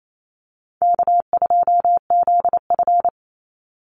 Morse code, heard as a keyed tone, about 28 wpm, 700 Hz. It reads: K2ZF
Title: Easy Call Signs Repeated 3x - 28wpm